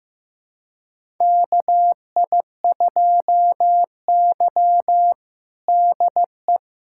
Morse code, heard as a keyed tone, about 15 wpm, 700 Hz. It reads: KI2Y DE
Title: USA Callsign DE Callsign - 4 Characters - 15wpm